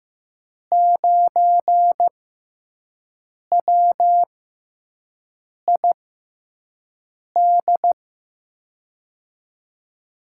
Morse code, heard as a keyed tone, about 15 wpm, spaced as for 5 wpm, 700 Hz. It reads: 9WID
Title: USA Call Signs - 6 Characters - 6x Character Spacing - 15wpm